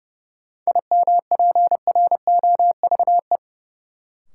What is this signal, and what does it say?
Morse code, heard as a keyed tone, about 30 wpm, 700 Hz. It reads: IMPROVE
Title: Sentences from Top 800 Words - New Word Drill - 30wpm